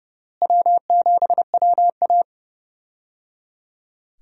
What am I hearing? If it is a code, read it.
Morse code, heard as a keyed tone, about 30 wpm, 700 Hz. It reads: W7WA
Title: USA Call Signs - 4 Characters - Repeated 3x - 30wpm